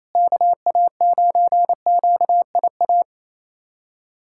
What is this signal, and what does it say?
Morse code, heard as a keyed tone, about 28 wpm, 700 Hz. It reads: KA9QIA
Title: USA Callsign DE Callsign 28wpm